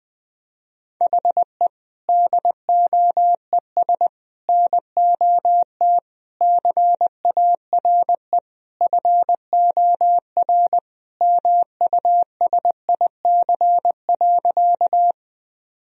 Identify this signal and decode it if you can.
Morse code, heard as a keyed tone, about 21 wpm, 700 Hz. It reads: HE DOES NOT CARE FOR MUSIC.
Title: Sentences from Top 1000 Words - Review Part 1 - 20wpm